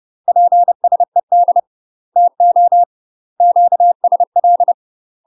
Morse code, heard as a keyed tone, about 30 wpm, 700 Hz. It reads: PSED TO QSL
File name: UK MASTER QSOs TRAINING / 30WPM / UK MASTER QSOs 0022